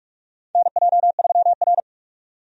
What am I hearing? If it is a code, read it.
Morse code, heard as a keyed tone, about 47 wpm, 700 Hz. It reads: NJ3R
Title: USA Call Signs - 4 Characters - 45wpm